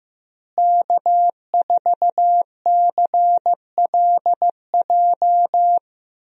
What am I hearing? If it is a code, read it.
Morse code, heard as a keyed tone, about 15 wpm, 700 Hz. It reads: K4CLJ
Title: USA Call Signs - 5 Characters - Repeated 3x - 15wpm